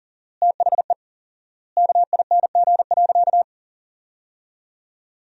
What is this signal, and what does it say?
Morse code, heard as a keyed tone, about 40 wpm, 700 Hz. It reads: THE KING.
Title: 4x Word-Spacing - Sentences from Top 400 Words - 40wpm